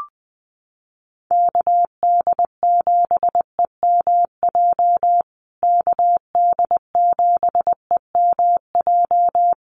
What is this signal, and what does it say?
Morse code, heard as a keyed tone, about 20 wpm, 700 Hz. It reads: KD7EMJ KD7EMJ
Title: USA Call Signs Repeated 3x 20wpm